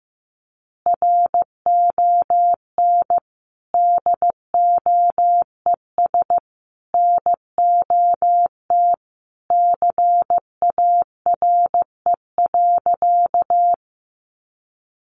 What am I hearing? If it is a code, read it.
Morse code, heard as a keyed tone, about 15 wpm, 700 Hz. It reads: RON DOES NOT CARE.
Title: Additional Easy Phrases 15wpm